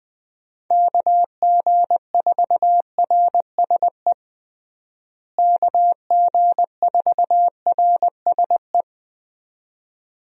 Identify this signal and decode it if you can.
Morse code, heard as a keyed tone, about 20 wpm, 700 Hz. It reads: KG4RSE KG4RSE
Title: USA Call Signs - 6 Characters - Repeated 2x - 20wpm